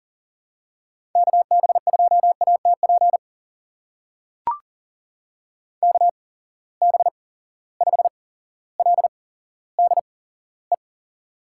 Morse code, heard as a keyed tone, about 38 wpm, spaced as for 17 wpm, 700 Hz. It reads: KB2ATP KB5LDE
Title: USA Call Signs - 8x Character Spacing - 40wpm